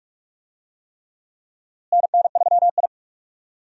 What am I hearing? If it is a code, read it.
Morse code, heard as a keyed tone, about 46 wpm, 700 Hz. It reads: NN3I
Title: USA Call Signs - 4 Characters - Repeated 3x - 45wpm